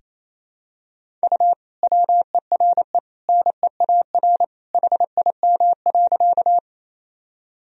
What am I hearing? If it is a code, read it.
Morse code, heard as a keyed tone, about 28 wpm, 700 Hz. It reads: U WERE NEAR HIM.
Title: Sentences from Top 200 Words and Abbreviations 28wpm